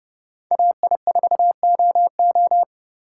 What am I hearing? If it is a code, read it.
Morse code, heard as a keyed tone, about 30 wpm, 700 Hz. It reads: AI4OO